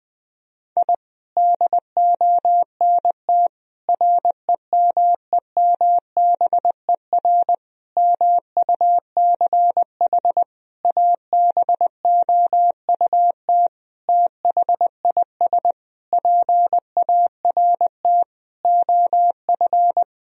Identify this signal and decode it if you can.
Morse code, heard as a keyed tone, about 21 wpm, 700 Hz. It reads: I DONT REMEMBER MUCH ABOUT THIS PART OF
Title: Sentences from Top 1000 Words - Review Part 1 - 20wpm